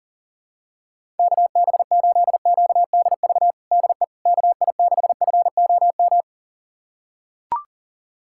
Morse code, heard as a keyed tone, about 40 wpm, 700 Hz. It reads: KB8QDV DE KI6FOM
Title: USA Callsign DE Callsign - 6 Characters - 40wpm